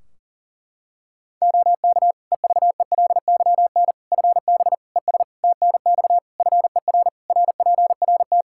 Morse code, heard as a keyed tone, about 40 wpm, 700 Hz. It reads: OK EVELYN FB ES TNX FER RPRT
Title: Practice QSO - Exchange 2 - Info - 40wpm